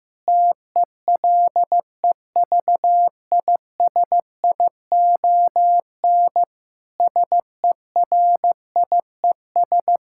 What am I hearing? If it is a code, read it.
Morse code, heard as a keyed tone, about 15 wpm, 700 Hz. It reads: TELEVISION SERIES
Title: Sentences from Top 1000 Words - Review Part 5 - 15wpm